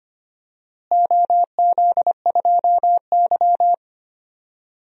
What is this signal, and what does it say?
Morse code, heard as a keyed tone, about 25 wpm, 700 Hz. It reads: OZ2Y
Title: DXCC-100 Call Signs - 4 Characters - Repeated 2x - 25wpm